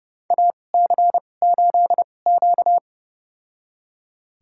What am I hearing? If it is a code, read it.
Morse code, heard as a keyed tone, about 30 wpm, spaced as for 22 wpm, 700 Hz. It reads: AC8Q
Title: USA Call Signs - 4 Characters - 2x Character Spacing - 30wpm